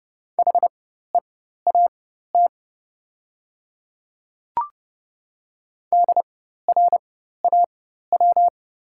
Morse code, heard as a keyed tone, about 30 wpm, spaced as for 14 wpm, 700 Hz. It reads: HEAT DRAW
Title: 4x Character Spacing - Top 500 Words - 30wpm